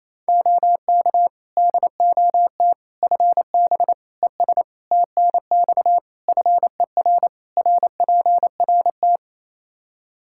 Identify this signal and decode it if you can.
Morse code, heard as a keyed tone, about 27 wpm, 700 Hz. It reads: OK DOT FB ES TNX FER RPRT
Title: Practice QSO - Exchange 2 - Info - 28wpm